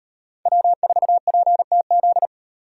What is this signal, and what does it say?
Morse code, heard as a keyed tone, about 36 wpm, 700 Hz. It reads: W4PTZ